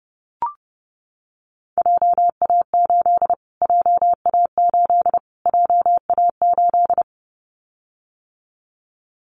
Morse code, heard as a keyed tone, about 30 wpm, 700 Hz. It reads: JA8 JA8 JA8
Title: Callsign Prefixes for Top 25 DXCC - Repeated 3x - 30wpm